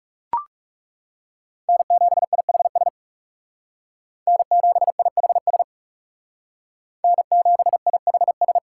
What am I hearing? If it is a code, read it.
Morse code, heard as a keyed tone, about 36 wpm, 700 Hz. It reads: N7IHS N7IHS N7IHS
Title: Speed Racing - USA 1x3 Call Signs 30wpm